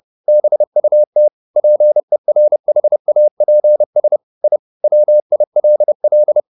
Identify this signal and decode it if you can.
Morse code, heard as a keyed tone, about 30 wpm, 600 Hz. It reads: BUT PERHAPS I WILL
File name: the old man and the sea / the old man and the sea 1 30wpm